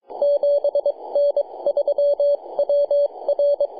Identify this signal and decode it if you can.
Morse code, heard as a keyed tone, about 22 wpm, spaced as for 19 wpm, 570 Hz. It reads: 7N3WR
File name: jg8nqj-jd1-21-076mhz-cw